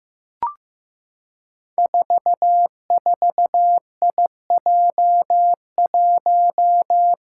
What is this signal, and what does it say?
Morse code, heard as a keyed tone, about 15 wpm, 700 Hz. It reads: 44IJ1
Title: Five Challenging Letter-Number Combinations 15wpm